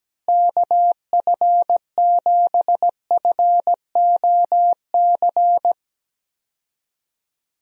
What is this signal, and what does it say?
Morse code, heard as a keyed tone, about 17 wpm, 700 Hz. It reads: KF7FOC
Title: USA Call Signs - 6 Characters - Repeated 2x - 17wpm